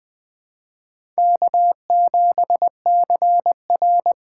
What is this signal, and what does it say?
Morse code, heard as a keyed tone, about 20 wpm, 700 Hz. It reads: K7CR
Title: USA Call Signs - 4 Characters - 3x Character Spacing - 20wpm